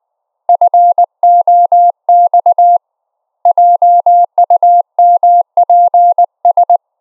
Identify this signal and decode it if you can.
Morse code, heard as a keyed tone, about 19 wpm, 700 Hz. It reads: FOX JUMPS
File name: Op E 700 Hz